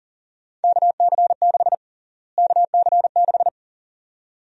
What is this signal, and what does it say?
Morse code, heard as a keyed tone, about 40 wpm, 700 Hz. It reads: KC6 KC6
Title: USA Call Sign Prefixes - Repeated 2x - 40wpm